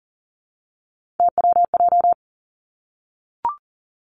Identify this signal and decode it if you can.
Morse code, heard as a keyed tone, about 39 wpm, 700 Hz. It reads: TWJ
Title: Three Challenging Letter-Number Combinations 40wpm